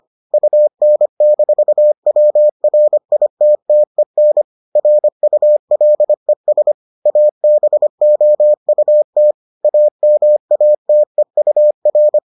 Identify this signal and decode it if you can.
Morse code, heard as a keyed tone, about 25 wpm, 600 Hz. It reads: UN-WRITTEN RULES ABOUT AMATEUR